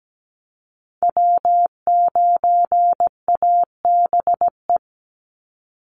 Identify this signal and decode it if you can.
Morse code, heard as a keyed tone, about 17 wpm, 700 Hz. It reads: W9ABE
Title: USA Call Signs 17wpm